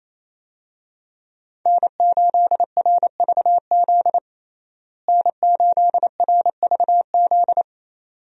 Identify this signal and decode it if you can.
Morse code, heard as a keyed tone, about 27 wpm, 700 Hz. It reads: N8RVZ N8RVZ
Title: USA Call Signs - 5 Characters - Repeated 2x - 28wpm